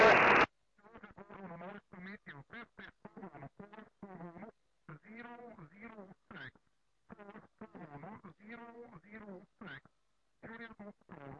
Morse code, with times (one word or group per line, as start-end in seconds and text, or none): none